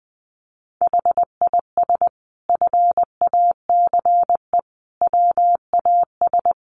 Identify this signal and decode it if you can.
Morse code, heard as a keyed tone, about 20 wpm, 700 Hz. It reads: HIS FACE WAS
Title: Sentences from Top 1200 Words 20wpm